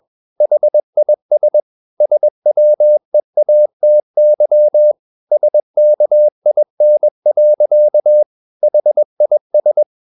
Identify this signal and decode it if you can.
Morse code, heard as a keyed tone, about 21 wpm, 600 Hz. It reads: HIS SWEATY SKIN. HIS
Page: CW med Gnister